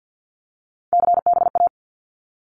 Morse code, heard as a keyed tone, about 50 wpm, 700 Hz. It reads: CBA
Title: Abbreviations Repeated 3x - 50wpm